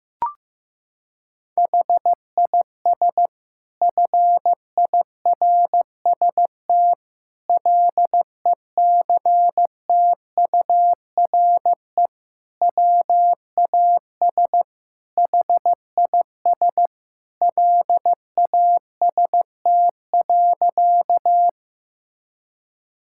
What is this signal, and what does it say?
Morse code, heard as a keyed tone, about 16 wpm, 700 Hz. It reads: HIS FIRST LECTURE WAS HIS LAST.
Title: Sentences from Top 1600 Words 15wpm